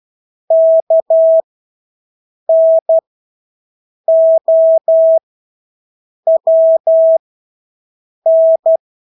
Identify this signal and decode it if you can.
Morse code, heard as a keyed tone, about 12 wpm, spaced as for 6 wpm, 650 Hz. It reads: KNOWN
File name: Engelske ord 12-6 wpm